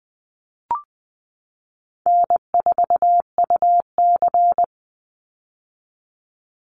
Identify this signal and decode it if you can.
Morse code, heard as a keyed tone, about 20 wpm, 700 Hz. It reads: N4UC
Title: Call Signs - 20wpm